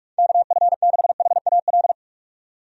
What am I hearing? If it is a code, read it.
Morse code, heard as a keyed tone, about 43 wpm, 700 Hz. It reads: KF6HAL
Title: USA Call Signs - 6 Characters - 45wpm